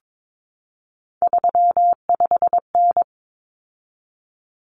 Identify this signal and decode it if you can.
Morse code, heard as a keyed tone, about 22 wpm, 700 Hz. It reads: 35N
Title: Common Cut Numbers - 3-Digit 22wpm